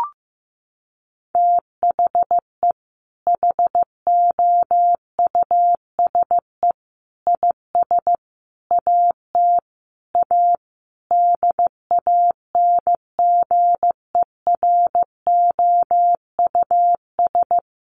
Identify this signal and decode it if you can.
Morse code, heard as a keyed tone, about 15 wpm, 700 Hz. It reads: THE HOUSE IS AT A DANGEROUS
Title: Sentences from Top 1500 Words 15wpm